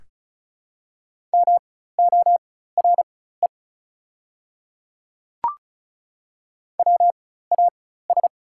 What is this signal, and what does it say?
Morse code, heard as a keyed tone, about 36 wpm, spaced as for 16 wpm, 690 Hz. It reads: MORE WAS
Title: 4x Character Spacing - Top 300 Words - 35wpm